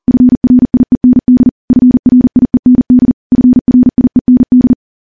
Morse code, heard as a keyed tone, about 40 wpm, 250 Hz. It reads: FRIEND FRIEND FRIEND
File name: morse / morse-training-40WPM